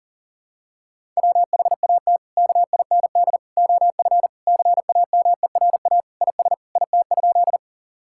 Morse code, heard as a keyed tone, about 37 wpm, 700 Hz. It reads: WHAT KIND OF CAMERA IS IT?
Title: Sentences from Top 1000 Words - Review Part 5 - 40wpm